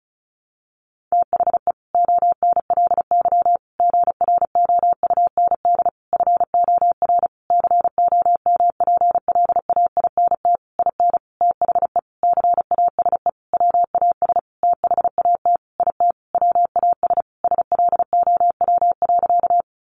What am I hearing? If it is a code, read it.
Morse code, heard as a keyed tone, about 34 wpm, 700 Hz. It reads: THE ONLY GROUND FOR COMPLAINT IN THE CASE WAS THAT IT WAS SLOW.
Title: Sentences from Top 1400 Words 35wpm